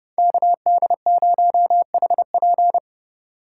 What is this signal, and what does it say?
Morse code, heard as a keyed tone, about 30 wpm, 700 Hz. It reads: KD0HP